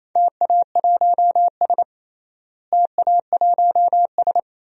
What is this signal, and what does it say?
Morse code, heard as a keyed tone, about 28 wpm, 700 Hz. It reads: TA1S TA1S
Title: DXCC-100 Call Signs - 4 Characters - Repeated 3x - 28wpm